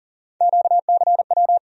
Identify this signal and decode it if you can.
Morse code, heard as a keyed tone, about 40 wpm, 700 Hz. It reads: QCW
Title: USA Call Sign Postfixes - 3 Letters 40wpm